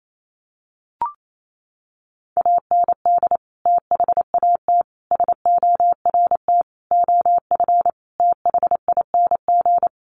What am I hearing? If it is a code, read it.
Morse code, heard as a keyed tone, about 28 wpm, 700 Hz. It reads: AND THAT SORT OF THING
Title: Sets of 5 Words Encore2 28wpm